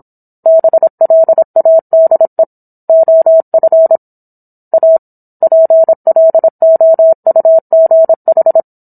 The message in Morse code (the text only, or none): BLADE OF A PLOUGH